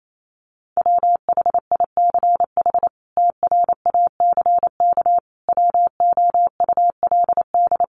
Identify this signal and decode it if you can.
Morse code, heard as a keyed tone, about 28 wpm, 700 Hz. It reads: WHICH TRACK WOULD
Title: Sentences from Top 1100 Words 28wpm